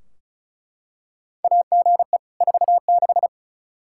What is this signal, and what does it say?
Morse code, heard as a keyed tone, about 36 wpm, 700 Hz. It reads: AGE 46
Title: Practice QSO - Exchange 3 - More Info - 35wpm